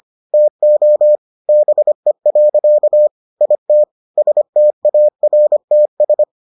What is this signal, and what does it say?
Morse code, heard as a keyed tone, about 25 wpm, 600 Hz. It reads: TO BE. IT STARTS